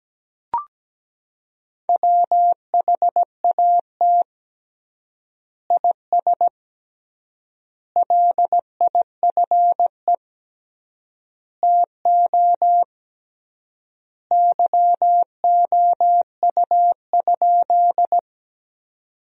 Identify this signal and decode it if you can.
Morse code, heard as a keyed tone, about 17 wpm, 700 Hz. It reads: WHAT IS LIFE TO YOU?
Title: 3x Word-Spacing - Sentences from Top 300 Words - 17wpm